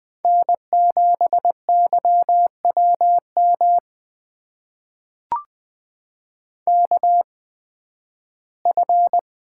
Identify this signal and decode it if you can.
Morse code, heard as a keyed tone, about 20 wpm, spaced as for 13 wpm, 700 Hz. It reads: N7YWM KF
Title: USA Call Signs - 5 Characters - 8x Character Spacing - 20wpm